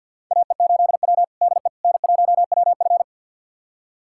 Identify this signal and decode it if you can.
Morse code, heard as a keyed tone, about 51 wpm, 700 Hz. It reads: AE8W DE N1WF
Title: USA Callsign DE Callsign - 4 Characters - 50wpm